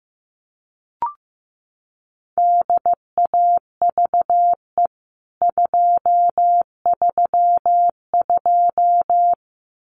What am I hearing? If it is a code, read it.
Morse code, heard as a keyed tone, about 15 wpm, 700 Hz. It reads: DAVE 232
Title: CWT Exchange 15wpm